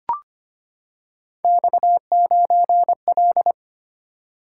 Morse code, heard as a keyed tone, about 25 wpm, 700 Hz. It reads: X9L
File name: Three Letter-Number Combinations - 25wpm